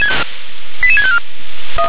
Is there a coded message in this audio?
Yes: five-tone selective call (ZVEI) from 0.8 to 1.2 s, heard as quick runs of five tones, tones of about 70 ms. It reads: 88654